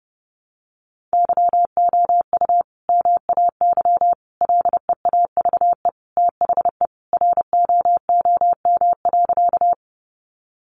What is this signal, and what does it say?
Morse code, heard as a keyed tone, about 30 wpm, 700 Hz. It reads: YOU MAY LEAVE THE ROOM.
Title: Sentences from Top 400 Words 30wpm